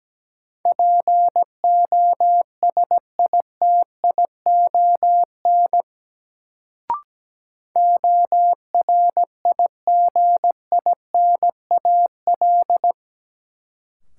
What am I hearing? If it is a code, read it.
Morse code, heard as a keyed tone, about 17 wpm, 700 Hz. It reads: POSITION ORIGINAL
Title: Sentences from Top 900 Words - New Word Drill - 17wpm